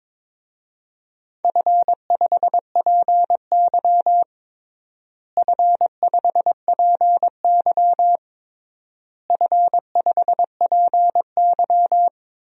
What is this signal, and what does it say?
Morse code, heard as a keyed tone, about 22 wpm, 700 Hz. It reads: F5PY F5PY F5PY
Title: DXCC-100 Call Signs - 4 Characters - Repeated 3x - 22wpm